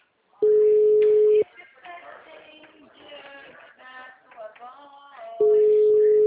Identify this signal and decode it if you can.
Ringback tone (Europe, 425 Hz), with speech.